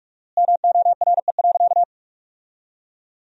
Morse code, heard as a keyed tone, about 41 wpm, 700 Hz. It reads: MORE.
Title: 4x Word-Spacing - Sentences from Top 500 Words - 45wpm